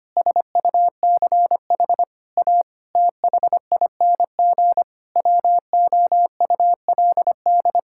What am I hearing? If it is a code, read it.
Morse code, heard as a keyed tone, about 25 wpm, 700 Hz. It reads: SUCH A THING WOULD
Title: Sentences from Top 1000 Words - Review Part 6 - 25wpm